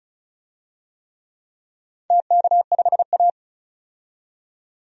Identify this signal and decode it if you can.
Morse code, heard as a keyed tone, about 36 wpm, 700 Hz. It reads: TK5A
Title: DXCC-100 Call Signs - 4 Characters - 35wpm